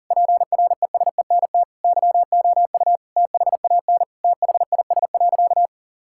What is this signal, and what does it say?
Morse code, heard as a keyed tone, about 40 wpm, 700 Hz. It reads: PRESENT YOU THAN THIS.
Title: Sentences from Top 1000 Words - Review Part 6 - 40wpm